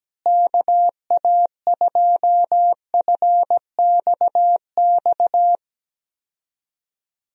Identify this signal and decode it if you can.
Morse code, heard as a keyed tone, about 17 wpm, 700 Hz. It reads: KA2FXX
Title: USA Call Signs - 6 Characters - Repeated 3x - 17wpm